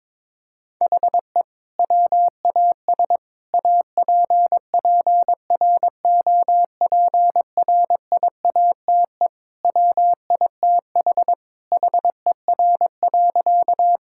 Morse code, heard as a keyed tone, about 22 wpm, 700 Hz. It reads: HE WAS APPROPRIATE WITH HER.
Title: Sentences from Top 1000 Words - Review Part 5 - 22wpm